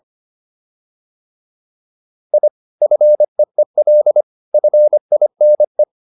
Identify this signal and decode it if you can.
Morse code, heard as a keyed tone, about 25 wpm, 600 Hz. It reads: I FEEL FINE